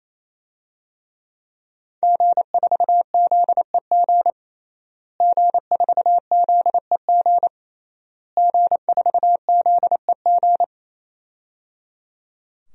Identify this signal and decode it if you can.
Morse code, heard as a keyed tone, about 28 wpm, 700 Hz. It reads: G4ZEG G4ZEG G4ZEG
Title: DXCC-100 Call Signs - 5 Characters - Repeated 3x - 28wpm